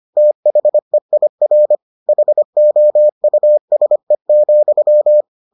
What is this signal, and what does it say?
Morse code, heard as a keyed tone, about 25 wpm, 600 Hz. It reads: THEIR HOUSE,